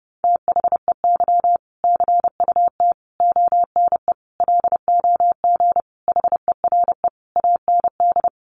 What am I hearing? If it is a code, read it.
Morse code, heard as a keyed tone, about 30 wpm, 700 Hz. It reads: THEY CUT ONE LOG HERE AND
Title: Sentences from Top 1400 Words 30wpm